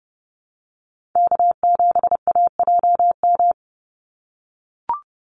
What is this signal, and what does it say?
Morse code, heard as a keyed tone, about 30 wpm, 700 Hz. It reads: K7AJM
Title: Call Signs - 30wpm